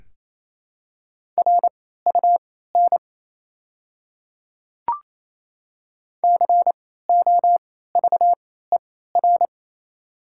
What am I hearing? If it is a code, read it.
Morse code, heard as a keyed tone, about 28 wpm, spaced as for 16 wpm, 700 Hz. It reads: RUN COVER
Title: 3x Character Spacing - Top 300 Words - 28wpm